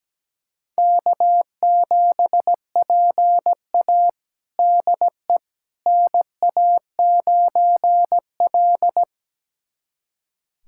Morse code, heard as a keyed tone, about 17 wpm, 700 Hz. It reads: K7PA DE NA9L
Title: USA Callsign DE Callsign - 4 Characters - 17wpm